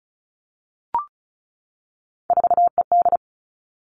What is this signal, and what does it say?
Morse code, heard as a keyed tone, about 36 wpm, 700 Hz. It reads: 4ED
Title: Three Challenging Letter-Number Combinations 35wpm